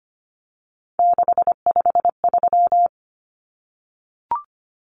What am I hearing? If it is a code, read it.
Morse code, heard as a keyed tone, about 25 wpm, 700 Hz. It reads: 653
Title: Numbers - 3-Digit 25wpm